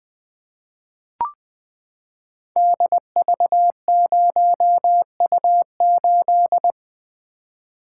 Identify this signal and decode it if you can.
Morse code, heard as a keyed tone, about 20 wpm, 700 Hz. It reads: DV0U8
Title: Code Groups - 5 Letter:Numbers - 20wpm